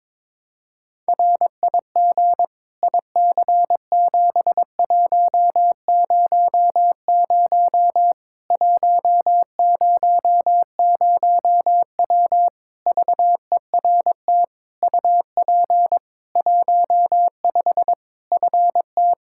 Morse code, heard as a keyed tone, about 22 wpm, 700 Hz. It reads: RIG IC7100 100W VERT UP 15 FT
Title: K1USN SST QSO Practice 22wpm